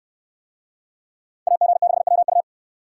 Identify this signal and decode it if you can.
Morse code, heard as a keyed tone, about 64 wpm, 700 Hz. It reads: AG6WU